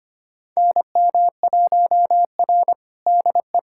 Morse code, heard as a keyed tone, about 25 wpm, 700 Hz. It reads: NM1R DE
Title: USA Callsign DE Callsign - 4 Characters - 25wpm